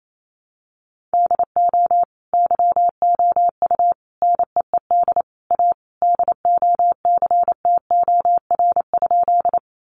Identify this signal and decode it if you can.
Morse code, heard as a keyed tone, about 28 wpm, 700 Hz. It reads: DO YOU NEED A DOCTOR?
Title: Sentences from Top 1500 Words 28wpm